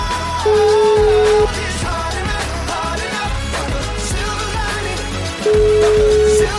Ringback tone (Europe, 425 Hz), with noise.